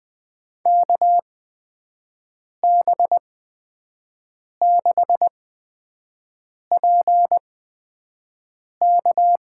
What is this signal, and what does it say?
Morse code, heard as a keyed tone, about 20 wpm, spaced as for 5 wpm, 700 Hz. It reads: KB6PK
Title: USA Call Signs - 8x Character Spacing - 20wpm